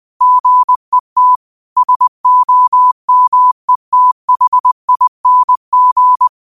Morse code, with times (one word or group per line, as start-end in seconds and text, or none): 0.2-1.4 GET
1.8-6.3 SOMETHING